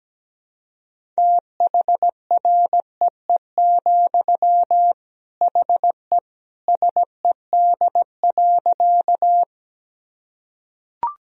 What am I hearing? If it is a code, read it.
Morse code, heard as a keyed tone, about 17 wpm, 700 Hz. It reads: THREE, HE SED.
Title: Sentences from Top 200 Words and Abbreviations 17wpm